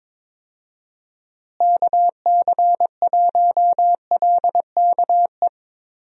Morse code, heard as a keyed tone, about 22 wpm, 700 Hz. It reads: KC1LKE
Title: USA Call Signs - 6 Characters - Repeated 3x - 22wpm